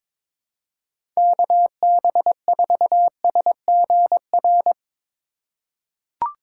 Morse code, heard as a keyed tone, about 22 wpm, 700 Hz. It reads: KB4SGR
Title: USA Call Signs - 4x Character Spacing - 22wpm